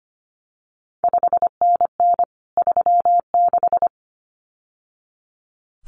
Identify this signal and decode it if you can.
Morse code, heard as a keyed tone, about 25 wpm, 700 Hz. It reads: 5NN 36
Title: CQ WW DX Contest Exchange Encore - 25wpm